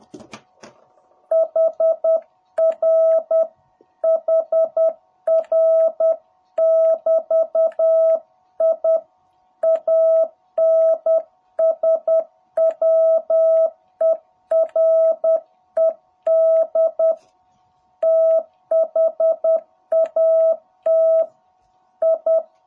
Morse code, heard as a keyed tone, about 10 wpm, 660 Hz. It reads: HR HR <BT> I ANSWERED THAT I
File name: jh8cbh / 2021cw